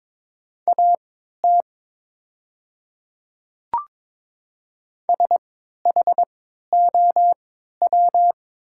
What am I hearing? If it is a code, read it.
Morse code, heard as a keyed tone, about 23 wpm, spaced as for 13 wpm, 700 Hz. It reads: AT SHOW
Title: 3x Character Spacing - Top 500 Words - 22wpm